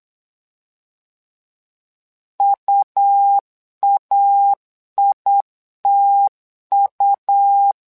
Morse code, heard as a keyed tone, about 8 wpm, 800 Hz. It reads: UAITU
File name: LEZIONI / 2 UHDN / 2 U 8wpm 2028